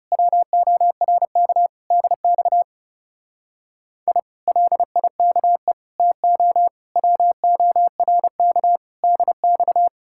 Morse code, heard as a keyed tone, about 34 wpm, 700 Hz. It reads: WORK DX I LIKE TO WORK DX
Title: Speed-Racing Common QSO Phrases Encore - 30wpm